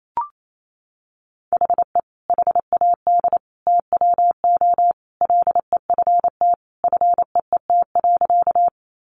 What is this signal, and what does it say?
Morse code, heard as a keyed tone, about 28 wpm, 700 Hz. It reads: HE HAD TWO LEFT FEET.
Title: Sentences from Top 300 Words 28wpm